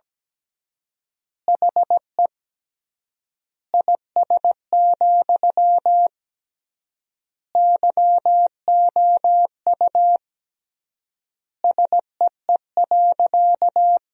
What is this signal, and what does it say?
Morse code, heard as a keyed tone, about 17 wpm, 700 Hz. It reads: HE IS, YOU SEE.
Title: 3x Word-Spacing - Sentences from Top 100 Words - 17wpm